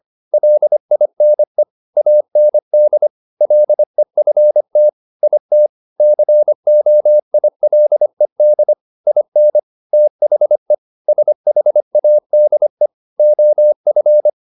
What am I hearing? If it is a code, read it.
Morse code, heard as a keyed tone, about 25 wpm, 600 Hz. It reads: LINE AND LEFT IT COILED IN THE SHADE OF
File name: the old man and the sea part 1 25wpm